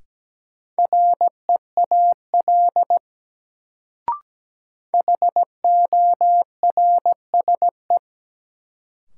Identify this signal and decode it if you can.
Morse code, heard as a keyed tone, about 17 wpm, 700 Hz. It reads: REAL HORSE
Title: Sentences from Top 300 Words - New Word Drill - 17wpm